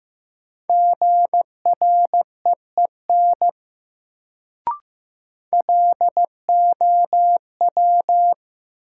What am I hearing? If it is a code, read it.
Morse code, heard as a keyed tone, about 15 wpm, 700 Hz. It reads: GREEN LOW
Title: Sentences from Top 500 Words - New Word Drill - 15wpm